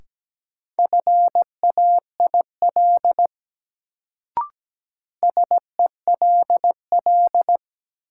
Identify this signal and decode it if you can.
Morse code, heard as a keyed tone, about 17 wpm, 700 Hz. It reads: FAIL SELL
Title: Sentences from Top 1100 Words - New Word Drill Alt - 17wpm